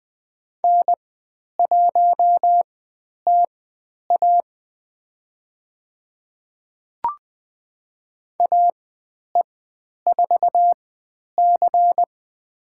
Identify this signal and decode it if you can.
Morse code, heard as a keyed tone, about 20 wpm, spaced as for 10 wpm, 700 Hz. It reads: N1TA AE4C
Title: 109 Lesson 18g - Letter m - Call Signs 20wpm